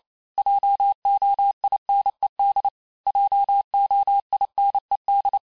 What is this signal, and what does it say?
Morse code, heard as a keyed tone, about 28 wpm, 800 Hz. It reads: JOINED JOINED
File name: cours-du-mercredi / audios / cours-865-teste-les-mots-vitesse-27mpm